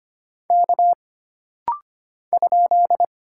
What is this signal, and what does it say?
Morse code, heard as a keyed tone, about 25 wpm, 700 Hz. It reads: K ?
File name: Single Character (Letter:Number:Punctuation:Prosign) - Rapid-Fire - 25wpm